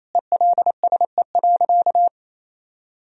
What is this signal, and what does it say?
Morse code, heard as a keyed tone, about 28 wpm, 700 Hz. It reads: ELSE.
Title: Sentences from Top 1000 Words - Review Part 1 - 28wpm